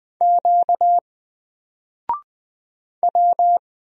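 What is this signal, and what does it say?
Morse code, heard as a keyed tone, about 20 wpm, 700 Hz. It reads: Q W
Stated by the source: Single letter and number Morse drill at 20 wpm, from N4LCD.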